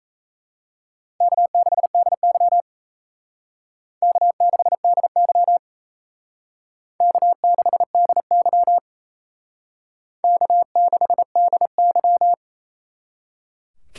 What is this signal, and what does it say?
Morse code, heard as a keyed tone, about 35 wpm, 700 Hz. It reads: K6DY K6DY K6DY K6DY
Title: Speed Racing - USA 1x2 Call Signs 28wpm